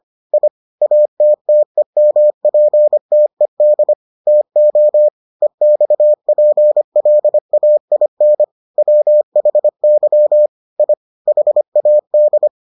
Morse code, heard as a keyed tone, about 25 wpm, 600 Hz. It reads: I ATTEMPTED TO EXPLAIN WHY I HAD